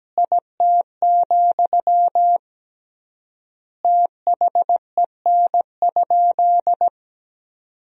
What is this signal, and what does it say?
Morse code, heard as a keyed tone, about 17 wpm, 700 Hz. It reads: IT, THEN?
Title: 3x Word-Spacing - Sentences from Top 200 Words - 17wpm